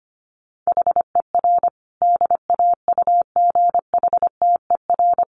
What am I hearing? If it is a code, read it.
Morse code, heard as a keyed tone, about 25 wpm, 700 Hz. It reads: HER DAUGHTER
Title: Sentences from Top 1500 Words 25wpm